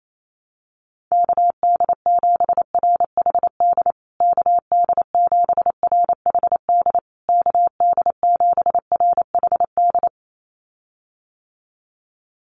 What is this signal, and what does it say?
Morse code, heard as a keyed tone, about 28 wpm, 700 Hz. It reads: KD7RHD KD7RHD KD7RHD
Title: USA Call Signs Repeated 3x 28wpm